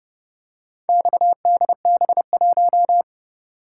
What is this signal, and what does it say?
Morse code, heard as a keyed tone, about 30 wpm, 700 Hz. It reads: XDB1